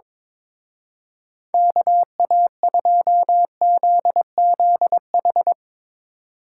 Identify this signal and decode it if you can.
Morse code, heard as a keyed tone, about 22 wpm, 700 Hz. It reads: KA2ZZH
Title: Speed Racing - USA 2x3 Call Signs 22wpm